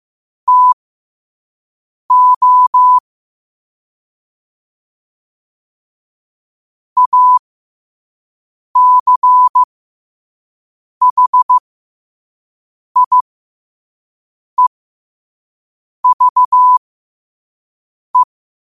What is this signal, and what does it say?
Morse code, heard as a keyed tone, about 15 wpm, spaced as for 5 wpm, 1000 Hz. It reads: TO ACHIEVE
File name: qotdfiles / 5wpm